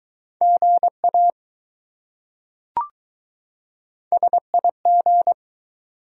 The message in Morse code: GA SIG

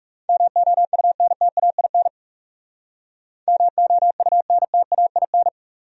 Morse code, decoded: MOUNTAIN MOUNTAIN

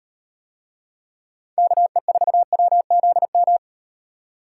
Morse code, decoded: KE4WZM

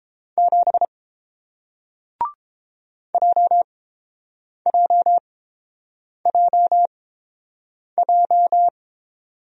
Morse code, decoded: 7 JJJJ

